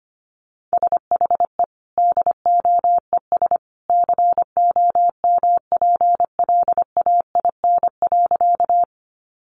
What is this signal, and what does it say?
Morse code, heard as a keyed tone, about 25 wpm, 700 Hz. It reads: SHE DOES COMPLAIN.
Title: Sentences from Top 1300 Words 25wpm